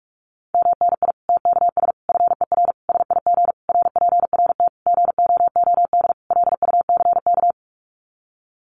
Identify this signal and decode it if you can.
Morse code, heard as a keyed tone, about 44 wpm, 700 Hz. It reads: MNI TKS FER SIG RPRT GOOD LUCK
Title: Common QSO Phrases 45wpm